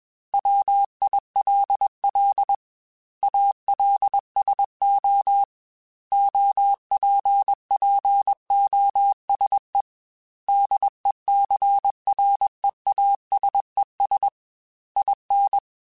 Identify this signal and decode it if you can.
Morse code, heard as a keyed tone, about 21 wpm, 800 Hz. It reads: WILL ALSO OPPOSE DECREASES IN
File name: Technical Subjects / Technical Subjects-mp3s / XC @20